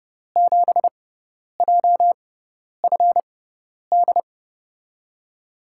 Morse code, heard as a keyed tone, about 30 wpm, spaced as for 10 wpm, 700 Hz. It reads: 7JFD
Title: USA Call Signs - 5 Characters - 6x Character Spacing - 30wpm